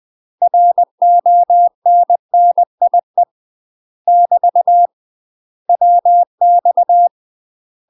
Morse code, heard as a keyed tone, about 20 wpm, 700 Hz. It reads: RONNIE <BT> WX
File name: UK MASTER QSOs TRAINING/20WPM